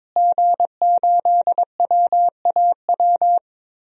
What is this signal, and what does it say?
Morse code, heard as a keyed tone, about 22 wpm, 700 Hz. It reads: G8WAW
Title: DXCC-100 Call Signs - 5 Characters - Repeated 3x - 22wpm